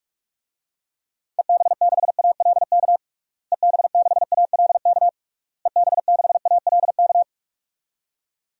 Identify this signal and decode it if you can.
Morse code, heard as a keyed tone, about 46 wpm, 700 Hz. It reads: EB6ALK EB6ALK EB6ALK
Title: DXCC-100 Call Signs - 6 Characters - Repeated 3x - 45wpm